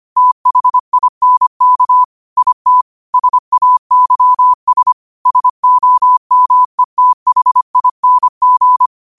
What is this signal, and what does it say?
Morse code, heard as a keyed tone, about 25 wpm, 1000 Hz. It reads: THINK IT SAYS SOMETHING
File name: qotdfiles / 25wpm